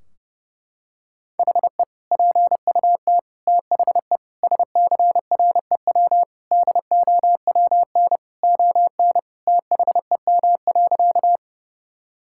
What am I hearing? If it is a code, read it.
Morse code, heard as a keyed tone, about 30 wpm, 700 Hz. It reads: HE PUT THE SCREW DOWN ON THEM.
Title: Sentences from Top 1600 Words 30wpm